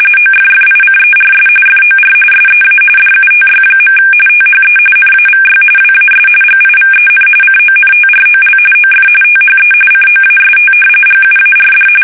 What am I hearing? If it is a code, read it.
RTTY, about 45.45 baud. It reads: OFFICERS FOR SEA DUTY. THE NAVY WAS GIVEN CONTROL OF THE COAST GUARD
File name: fsk850